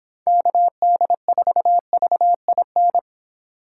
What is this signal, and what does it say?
Morse code, heard as a keyed tone, about 26 wpm, 700 Hz. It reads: KD4VIN